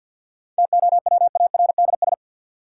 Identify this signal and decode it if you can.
Morse code, heard as a keyed tone, about 49 wpm, 700 Hz. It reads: TOWARDS
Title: Sentences from Top 1100 Words - New Word Drill Alt - 50wpm